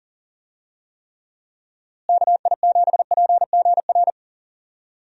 Morse code, heard as a keyed tone, about 40 wpm, 700 Hz. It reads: KI7PGR
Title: USA Call Signs - 6 Characters - Repeated 3x - 40wpm